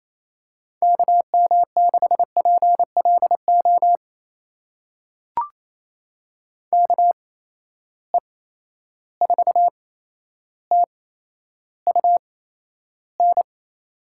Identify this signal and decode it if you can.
Morse code, heard as a keyed tone, about 27 wpm, spaced as for 12 wpm, 700 Hz. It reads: KM6PLO KE4TUN
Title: USA Call Signs - 6 Characters - 8x Character Spacing - 28wpm